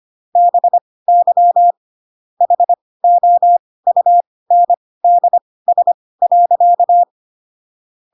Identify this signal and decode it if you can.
Morse code, heard as a keyed tone, about 25 wpm, spaced as for 18 wpm, 700 Hz. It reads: BY HOUNDS.
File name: Pinocchio / Pinocchio ch4 12 25 ik1hge